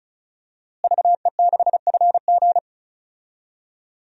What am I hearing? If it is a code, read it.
Morse code, heard as a keyed tone, about 36 wpm, 700 Hz. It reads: VE6FG
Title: DXCC-100 Call Signs - 5 Characters - Repeated 3x - 35wpm